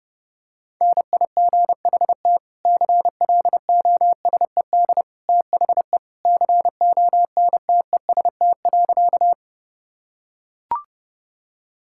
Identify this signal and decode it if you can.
Morse code, heard as a keyed tone, about 30 wpm, 700 Hz. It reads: NIGHT CLOSED THE CONTEST.
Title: Sentences from Top 1600 Words 30wpm